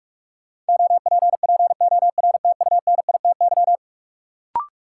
Morse code, heard as a keyed tone, about 46 wpm, 700 Hz. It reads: OPPORTUNITY
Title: Sentences from Top 1000 Words - New Word Drill - 45wpm